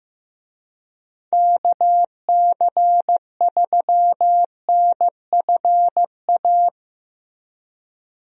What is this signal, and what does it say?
Morse code, heard as a keyed tone, about 15 wpm, 700 Hz. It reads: KC3NFA
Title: USA Call Signs - 6 Characters - Repeated 2x - 15wpm